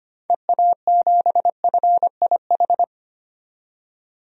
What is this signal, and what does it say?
Morse code, heard as a keyed tone, about 25 wpm, 700 Hz. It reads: EA7FIH
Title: DXCC-100 Call Signs - 6 Characters - 25wpm